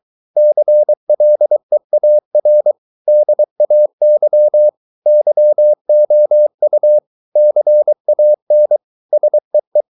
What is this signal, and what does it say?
Morse code, heard as a keyed tone, about 23 wpm, 600 Hz. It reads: CLEAR DAY YOU CAN SEE